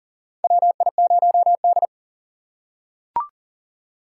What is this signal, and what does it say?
Morse code, heard as a keyed tone, about 40 wpm, 700 Hz. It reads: WI0D
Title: USA Call Signs - 4 Characters - 6x Character Spacing - 40wpm